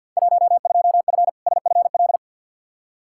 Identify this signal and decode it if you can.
Morse code, heard as a keyed tone, about 52 wpm, 700 Hz. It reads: 12F SFL
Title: Field Day Exchanges Encore - 50wpm